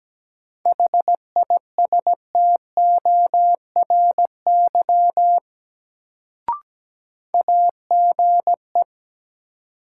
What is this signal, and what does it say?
Morse code, heard as a keyed tone, about 17 wpm, 700 Hz. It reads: HISTORY AGE
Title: Sets of 3 Words - New Word Drill - 17wpm